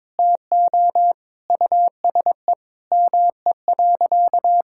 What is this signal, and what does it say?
Morse code, heard as a keyed tone, about 22 wpm, 700 Hz. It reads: TO USE ME.
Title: Sentences from Top 1000 Words - Review Part 6 - 22wpm